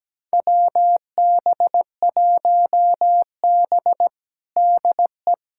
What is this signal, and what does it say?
Morse code, heard as a keyed tone, about 17 wpm, 700 Hz. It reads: WB1B DE
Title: USA Callsign DE Callsign - 4 Characters - 17wpm